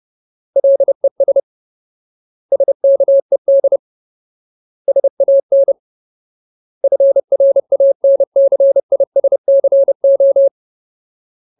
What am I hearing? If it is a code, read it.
Morse code, heard as a keyed tone, about 30 wpm, 550 Hz. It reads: LES SKED SAN FRANCISCO
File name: Ultimate Word List / 30WPM Ultimate